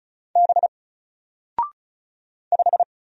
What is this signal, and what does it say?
Morse code, heard as a keyed tone, about 36 wpm, 700 Hz. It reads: B 5
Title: Single Letter-Number 35wpm